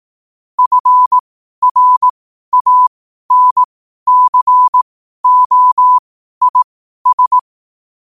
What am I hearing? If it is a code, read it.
Morse code, heard as a keyed tone, about 18 wpm, spaced as for 12 wpm, 1000 Hz. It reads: FRANCOIS
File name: 12wpm